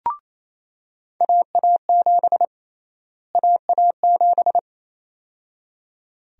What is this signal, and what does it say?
Morse code, heard as a keyed tone, about 28 wpm, 700 Hz. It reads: AA7 AA7
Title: USA Call Sign Prefixes - Repeated 2x - 28wpm